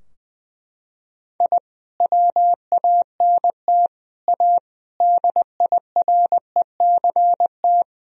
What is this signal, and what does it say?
Morse code, heard as a keyed tone, about 20 wpm, 700 Hz. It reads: I WANT A DIRECT
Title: Sentences from Top 1000 Words - Review Part 2 - 20wpm